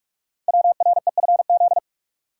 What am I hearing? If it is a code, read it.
Morse code, heard as a keyed tone, about 44 wpm, 700 Hz. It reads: WREFZ
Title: Code Groups - 5 Letter:Numbers - 45wpm